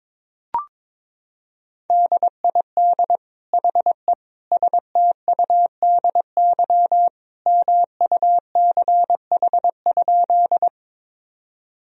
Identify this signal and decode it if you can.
Morse code, heard as a keyed tone, about 22 wpm, 700 Hz. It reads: DID HE STUDY MUCH?
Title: Sentences from Top 1000 Words - Review Part 1 - 22wpm